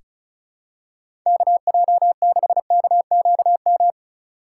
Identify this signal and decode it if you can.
Morse code, heard as a keyed tone, about 36 wpm, 700 Hz. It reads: KJ6KQM
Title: USA Call Signs - 4x Character Spacing - 35wpm